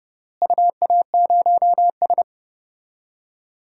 Morse code, heard as a keyed tone, about 30 wpm, 700 Hz. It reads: UA0S
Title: DXCC-100 Call Signs - 4 Characters - Repeated 3x - 30wpm